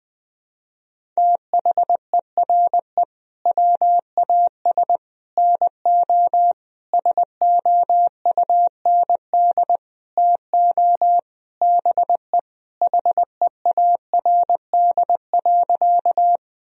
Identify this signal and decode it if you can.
Morse code, heard as a keyed tone, about 21 wpm, 700 Hz. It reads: THERE WAS NO SOUND TO BE HEARD.
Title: Sentences from Top 1000 Words - Review Part 1 - 20wpm